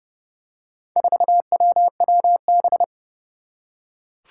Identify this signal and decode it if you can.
Morse code, heard as a keyed tone, about 30 wpm, 700 Hz. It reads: 4WWB
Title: Four Challenging Letter-Number Combinations 30wpm